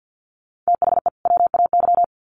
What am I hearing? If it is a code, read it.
Morse code, heard as a keyed tone, about 50 wpm, 700 Hz. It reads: THE WAY
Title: Sets of 2 Words 50wpm